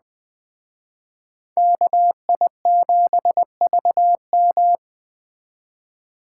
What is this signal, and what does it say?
Morse code, heard as a keyed tone, about 20 wpm, 700 Hz. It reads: KI7VM
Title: Speed Racing - USA Call Signs 17wpm